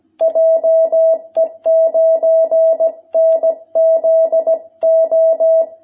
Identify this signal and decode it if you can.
Morse code, heard as a keyed tone, about 17 wpm, 660 Hz. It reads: JE9NZO
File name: jh8cbh / 2021cw